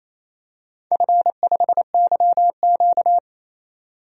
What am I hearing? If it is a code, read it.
Morse code, heard as a keyed tone, about 28 wpm, 700 Hz. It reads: F5YQ
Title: DXCC-100 Call Signs - 4 Characters - 28wpm